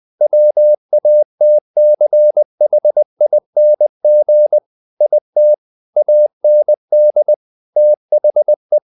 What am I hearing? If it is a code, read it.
Morse code, heard as a keyed tone, about 20 wpm, 600 Hz. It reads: WATCHING IT AND THE
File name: the old man and the sea / the old man and the sea part 1 20wpm